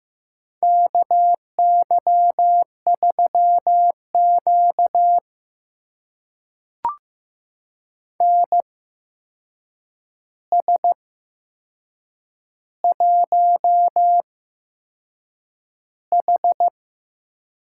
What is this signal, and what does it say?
Morse code, heard as a keyed tone, about 15 wpm, spaced as for 6 wpm, 700 Hz. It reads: KY3Q NS1H
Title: USA Call Signs - 4 Characters - 8x Character Spacing - 15wpm